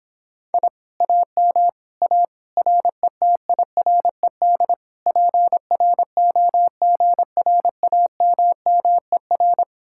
Morse code, heard as a keyed tone, about 29 wpm, 700 Hz. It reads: I AM A RETIRED PROGRAMMER